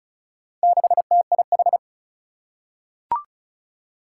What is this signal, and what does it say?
Morse code, heard as a keyed tone, about 36 wpm, 700 Hz. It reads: 6TIH
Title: Four Challenging Letter-Number Combinations 35wpm